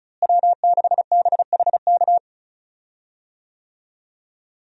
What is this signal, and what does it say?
Morse code, heard as a keyed tone, about 34 wpm, 700 Hz. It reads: W6BHK
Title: USA Call Signs - 5 Characters - Repeated 2x - 35wpm